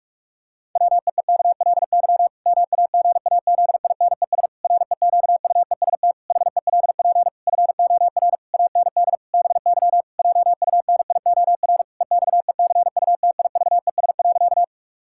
Morse code, heard as a keyed tone, about 46 wpm, 700 Hz. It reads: WEEKLY MAGAZINES REQUEST HELP FOR AND BY JUNIOR EXECUTIVES.
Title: Pangram Challenge v2 - 45wpm